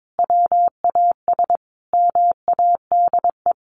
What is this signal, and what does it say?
Morse code, heard as a keyed tone, about 22 wpm, 700 Hz. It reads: WAS MADE
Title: Sentences from Top 1500 Words 22wpm